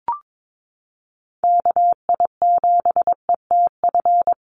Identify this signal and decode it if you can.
Morse code, heard as a keyed tone, about 22 wpm, 700 Hz. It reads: KI7ETF
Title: USA Call Signs Repeated 3x 22wpm